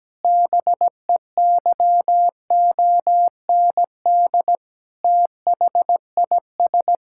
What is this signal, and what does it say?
Morse code, heard as a keyed tone, about 17 wpm, 700 Hz. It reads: BEYOND THIS
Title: Sentences from Top 1000 Words - Review Part 6 - 17wpm